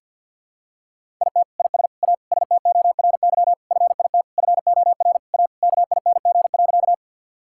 Morse code, heard as a keyed tone, about 50 wpm, 690 Hz. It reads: IT IS A STORY FIT FOR A KING.
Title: Sentences from Top 1000 Words - Review Part 5 - 50wpm